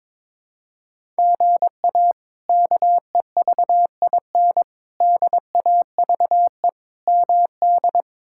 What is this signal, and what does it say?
Morse code, heard as a keyed tone, about 22 wpm, 700 Hz. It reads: GA KEVIN DAVE MD
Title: K1USN SST QSO Practice - 22wpm